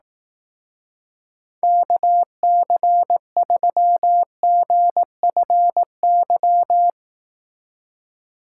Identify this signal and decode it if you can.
Morse code, heard as a keyed tone, about 18 wpm, 700 Hz. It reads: KC3GFY
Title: Speed Racing - USA 2x3 Call Signs 15wpm